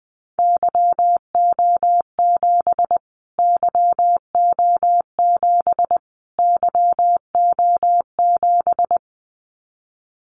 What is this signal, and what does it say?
Morse code, heard as a keyed tone, about 20 wpm, 700 Hz. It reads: YO7 YO7 YO7
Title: Callsign Prefixes for Top 25 DXCC - Repeated 3x - 20wpm